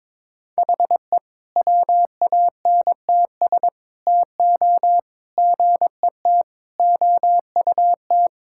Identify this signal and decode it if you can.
Morse code, heard as a keyed tone, about 22 wpm, 700 Hz. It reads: HE WANTS TO GET OUT
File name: practice / 22wpm / Sentences with 4-Word Phrases - 22wpm